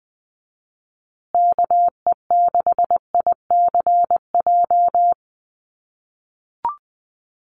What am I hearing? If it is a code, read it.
Morse code, heard as a keyed tone, about 20 wpm, 700 Hz. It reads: KE6ICJ
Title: USA Call Signs Repeated 2x 20wpm